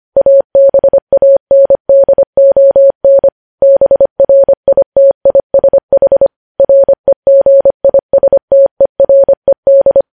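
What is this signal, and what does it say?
Morse code, heard as a keyed tone, about 25 wpm, 550 Hz. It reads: ABANDON BRITISH REGISTERED
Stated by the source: Morse code news headlines at 25 words per minute.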